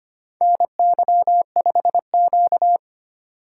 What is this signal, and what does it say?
Morse code, heard as a keyed tone, about 25 wpm, 700 Hz. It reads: NY5Q
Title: USA Call Signs - 4 Characters - 4x Character Spacing - 25wpm